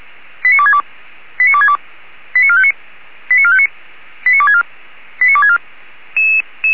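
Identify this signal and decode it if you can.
Five-tone selective call (ZVEI), heard as quick runs of five tones, tones of about 70 ms. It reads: 89262 89262 89459 89459 89264 89264
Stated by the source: ZVEI five-tone selective call 89411 received on SDR, 464 channel.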